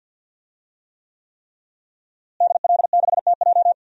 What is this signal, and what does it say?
Morse code, heard as a keyed tone, about 52 wpm, 700 Hz. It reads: DL6TJ
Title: DXCC-100 Call Signs - 5 Characters - Repeated 3x - 50wpm